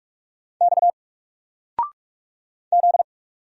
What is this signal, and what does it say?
Morse code, heard as a keyed tone, about 44 wpm, 700 Hz. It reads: X Z
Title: Single Letter-Number 45wpm